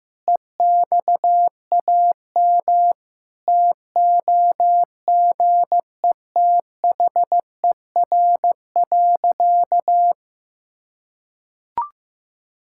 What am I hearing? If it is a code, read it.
Morse code, heard as a keyed tone, about 16 wpm, 700 Hz. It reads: EXAM TOGETHER.
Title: Sentences from Top 1000 Words - Review Part 5 - 15wpm